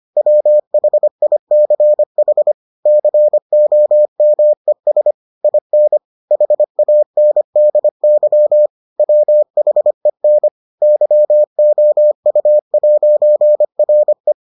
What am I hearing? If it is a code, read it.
Morse code, heard as a keyed tone, about 25 wpm, 600 Hz. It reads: WHICH COMES IN HANDY WHEN YOU'RE